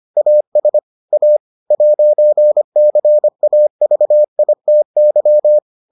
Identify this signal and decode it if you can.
Morse code, heard as a keyed tone, about 25 wpm, 600 Hz. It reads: AS A 'CAVITY